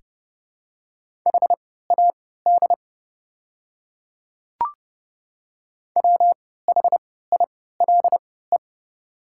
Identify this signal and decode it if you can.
Morse code, heard as a keyed tone, about 30 wpm, spaced as for 17 wpm, 700 Hz. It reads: HAD WHILE
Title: 3x Character Spacing - Top 300 Words - 30wpm